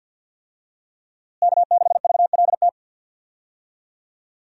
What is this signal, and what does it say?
Morse code, heard as a keyed tone, about 52 wpm, 700 Hz. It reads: K6VLT
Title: USA Call Signs - 5 Characters - Repeated 3x - 50wpm